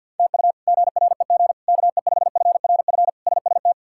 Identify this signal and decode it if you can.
Morse code, heard as a keyed tone, about 52 wpm, 700 Hz. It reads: TU GREG KE5FRF SST
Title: K1USN SST QSO Practice - 50wpm